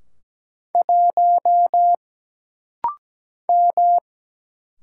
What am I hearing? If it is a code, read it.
Morse code, heard as a keyed tone, about 17 wpm, 700 Hz. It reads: 1 M
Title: Single Letter-Number - Rapid-Fire 17wpm